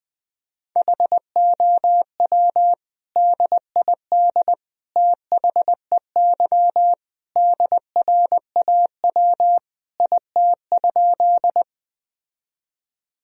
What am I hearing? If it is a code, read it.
Morse code, heard as a keyed tone, about 20 wpm, 700 Hz. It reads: HOW DID THEY DRAW IT?
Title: Sentences from Top 1000 Words - Review Part 1 - 20wpm